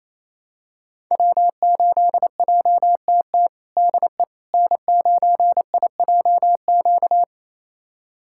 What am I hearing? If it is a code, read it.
Morse code, heard as a keyed tone, about 28 wpm, 700 Hz. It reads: W8JTT DE N9IJQ
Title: USA Callsign DE Callsign - 5 Characters - 28wpm